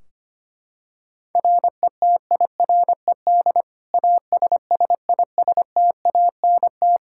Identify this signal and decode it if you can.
Morse code, heard as a keyed tone, about 25 wpm, 700 Hz. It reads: RETIRED ASSISTANT
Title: Practice QSO - Exchange 3 - More Info - 25wpm